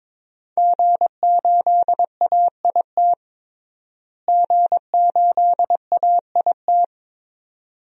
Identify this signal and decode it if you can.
Morse code, heard as a keyed tone, about 21 wpm, 700 Hz. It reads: G8AIT G8AIT
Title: DXCC-100 Call Signs - 5 Characters - Repeated 3x - 22wpm